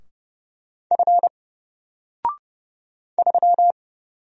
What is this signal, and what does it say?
Morse code, heard as a keyed tone, about 29 wpm, 700 Hz. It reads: F 3
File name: Single-Letter-Number-30wpm